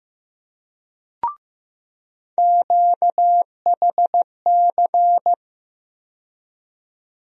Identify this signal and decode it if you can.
Morse code, heard as a keyed tone, about 15 wpm, 700 Hz. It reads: QHC
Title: USA Call Sign Postfixes - 3 Letters 15wpm